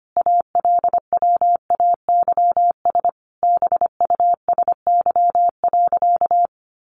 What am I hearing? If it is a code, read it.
Morse code, heard as a keyed tone, about 25 wpm, 700 Hz. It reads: ALWAYS BUSY.
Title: Additional Easy Phrases 25wpm